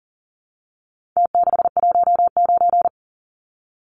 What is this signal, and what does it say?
Morse code, heard as a keyed tone, about 37 wpm, 700 Hz. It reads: T619
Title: Numbers - Contest Style - 4-Digit 40wpm